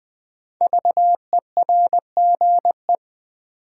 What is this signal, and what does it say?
Morse code, heard as a keyed tone, about 20 wpm, 700 Hz. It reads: VERGE
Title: Sets of 4 Words - New Word Drill - 20wpm